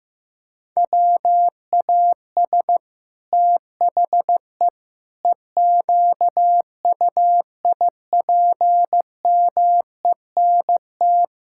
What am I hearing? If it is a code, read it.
Morse code, heard as a keyed tone, about 15 wpm, 700 Hz. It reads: WAS THE EQUIPMENT
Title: Sentences from Top 1000 Words - Review Part 5 - 15wpm